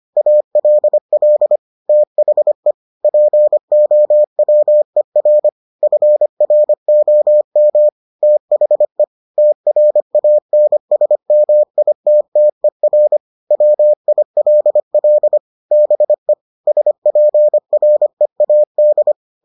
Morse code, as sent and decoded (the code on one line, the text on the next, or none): .- .-.. .-.. / - .... / .--. --- .-- . .-. / ..-. .-. --- -- / - .... / - .-. .- -. ... -- .. - - . .-. / .-- .. .-.. .-.. / -... / ... .--. .-. .- -..
ALL THE POWER FROM THE TRANSMITTER WILL BE SPREAD